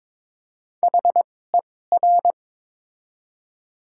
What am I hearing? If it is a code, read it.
Morse code, heard as a keyed tone, about 21 wpm, spaced as for 16 wpm, 700 Hz. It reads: HER
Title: 2x Character Spacing - Top 300 Words - 22wpm